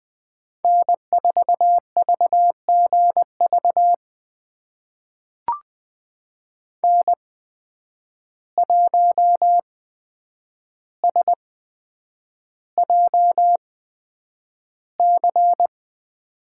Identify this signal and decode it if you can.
Morse code, heard as a keyed tone, about 20 wpm, spaced as for 9 wpm, 700 Hz. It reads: N4VGV N1SJC
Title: USA Call Signs - 5 Characters - 8x Character Spacing - 20wpm